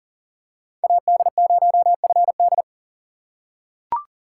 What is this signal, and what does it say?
Morse code, heard as a keyed tone, about 40 wpm, 700 Hz. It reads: AD0FD
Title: USA Call Signs - 5 Characters - 2x Character Spacing - 40wpm